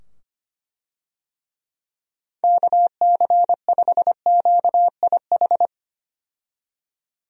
Morse code, heard as a keyed tone, about 25 wpm, 700 Hz. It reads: KC5QIH
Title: USA Call Signs - 6 Characters - 25wpm